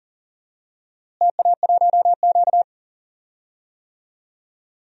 Morse code, heard as a keyed tone, about 39 wpm, 700 Hz. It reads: TA1Q
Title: DXCC-100 Call Signs - 4 Characters - Repeated 2x - 40wpm